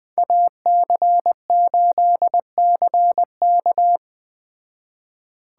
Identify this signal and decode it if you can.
Morse code, heard as a keyed tone, about 20 wpm, 700 Hz. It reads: AC8CK